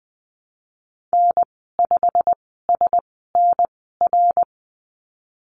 Morse code, heard as a keyed tone, about 20 wpm, spaced as for 15 wpm, 700 Hz. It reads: N5SNR
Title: 122 Lesson 20g - Number 3 - Call Signs 20wpm